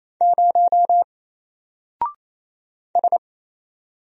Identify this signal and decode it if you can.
Morse code, heard as a keyed tone, about 27 wpm, 700 Hz. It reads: 0 S
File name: Single-Letter-Number-28wpm